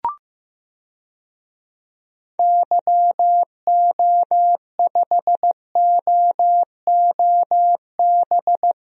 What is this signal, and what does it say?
Morse code, heard as a keyed tone, about 15 wpm, 700 Hz. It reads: YO5OOB
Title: DXCC-100 Call Signs - 6 Characters - Repeated 3x - 15wpm